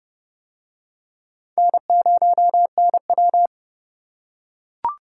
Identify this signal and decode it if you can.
Morse code, heard as a keyed tone, about 26 wpm, 700 Hz. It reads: N0NW